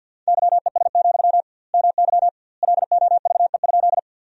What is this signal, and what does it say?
Morse code, heard as a keyed tone, about 50 wpm, 700 Hz. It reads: YES, MY LOVE?
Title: 2x Word-Spacing - Sentences from Top 500 Words - 50wpm